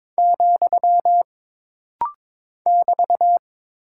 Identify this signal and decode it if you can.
Morse code, heard as a keyed tone, about 22 wpm, 700 Hz. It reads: , <BT>
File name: practice / 22wpm / Single Punctuation-Prosign - Rapid-Fire - 22wpm